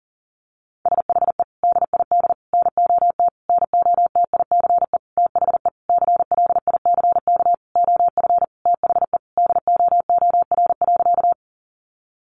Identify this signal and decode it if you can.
Morse code, heard as a keyed tone, about 40 wpm, 700 Hz. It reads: SHE DID NOT NOTICE THE CLICK OF THE DOOR.
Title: Sentences from Top 1100 Words 40wpm